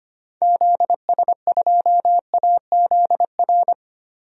Morse code, heard as a keyed tone, about 25 wpm, 700 Hz. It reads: ZS2AZR
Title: DXCC-100 Call Signs - 6 Characters - 25wpm